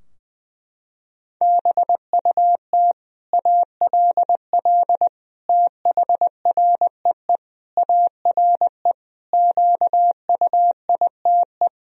Morse code, heard as a keyed tone, about 20 wpm, 700 Hz. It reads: BUT ALL THREE ARE QUITE
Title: Sentences from Top 1000 Words - Review Part 3 - 20wpm